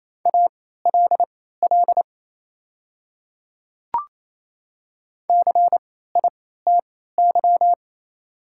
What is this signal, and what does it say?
Morse code, heard as a keyed tone, about 28 wpm, spaced as for 16 wpm, 700 Hz. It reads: ALL CITY
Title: 3x Character Spacing - Top 300 Words - 28wpm